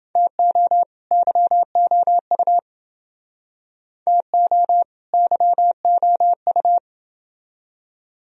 Morse code, heard as a keyed tone, about 31 wpm, 700 Hz. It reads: TO YOU TO YOU